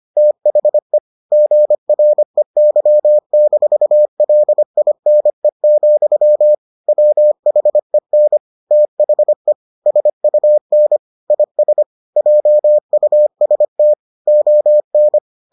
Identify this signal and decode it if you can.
Morse code, heard as a keyed tone, about 25 wpm, 600 Hz. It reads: THE GREY-LINE, WHEN THE SUN IS JUST ON